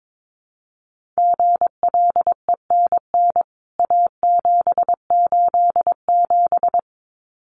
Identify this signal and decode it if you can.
Morse code, heard as a keyed tone, about 22 wpm, 700 Hz. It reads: GLENN A787
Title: CWT Exchange 22wpm